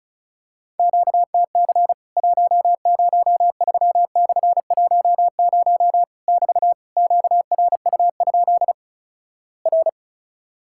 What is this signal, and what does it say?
Morse code, heard as a keyed tone, about 36 wpm, 700 Hz. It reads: QTC 103/10 <BT> QRU? R
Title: Worked All Europe DX Contest - QTC Exchange - 35wpm